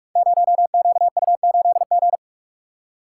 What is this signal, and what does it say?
Morse code, heard as a keyed tone, about 44 wpm, 700 Hz. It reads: 0QU8G
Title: Code Groups - 5 Letter:Numbers - 45wpm